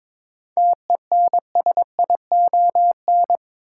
Morse code, heard as a keyed tone, about 22 wpm, 700 Hz. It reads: TENSION
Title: Sentences from Top 1600 Words - New Word Drill Alt - 22wpm